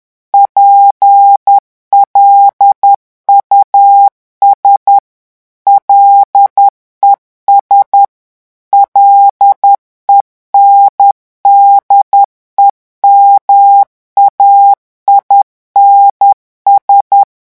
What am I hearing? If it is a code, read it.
Morse code, heard as a keyed tone, about 10 wpm, 800 Hz. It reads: PLUS LES LENDEMAINS
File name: revision9 citations humoristiques vitesse 10 mots minute